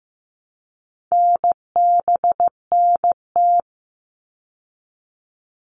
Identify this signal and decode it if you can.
Morse code, heard as a keyed tone, about 15 wpm, 700 Hz. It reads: NBNT
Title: All Cut Numbers - 4-Digit 15wpm